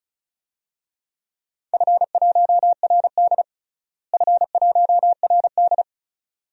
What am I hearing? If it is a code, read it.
Morse code, heard as a keyed tone, about 36 wpm, 700 Hz. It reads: F1RD F1RD
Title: DXCC-100 Call Signs - 4 Characters - Repeated 3x - 35wpm